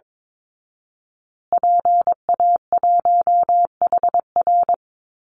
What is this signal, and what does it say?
Morse code, heard as a keyed tone, about 22 wpm, 700 Hz. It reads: PA1HR
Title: Call Signs Repeated 2x - 22wpm